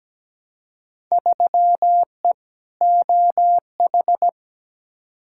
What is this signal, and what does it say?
Morse code, heard as a keyed tone, about 17 wpm, 700 Hz. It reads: 3E OH
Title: Field Day Exchanges - 17wpm